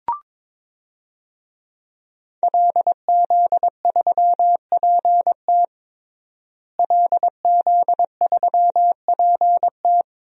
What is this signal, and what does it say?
Morse code, heard as a keyed tone, about 22 wpm, 700 Hz. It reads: LZ3PT LZ3PT
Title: DXCC-100 Call Signs - 5 Characters - Repeated 3x - 22wpm